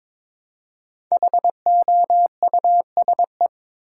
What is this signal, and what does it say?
Morse code, heard as a keyed tone, about 22 wpm, 700 Hz. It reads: HOUSE
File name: 5 Letter Words - First 3 Letters Given - 22wpm